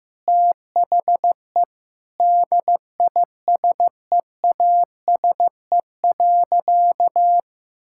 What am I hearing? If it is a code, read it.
Morse code, heard as a keyed tone, about 15 wpm, 700 Hz. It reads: THE DISEASE.
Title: Sentences from Top 1000 Words - Review Part 6 - 15wpm